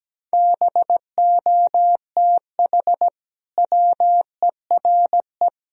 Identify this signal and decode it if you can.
Morse code, heard as a keyed tone, about 17 wpm, 700 Hz. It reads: BOTH WERE
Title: Sentences from Top 1000 Words - Review Part 2 - 17wpm